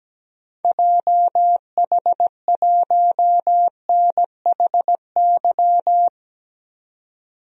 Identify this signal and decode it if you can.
Morse code, heard as a keyed tone, about 17 wpm, 700 Hz. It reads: JH1NHY
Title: DXCC-100 Call Signs - 6 Characters - 17wpm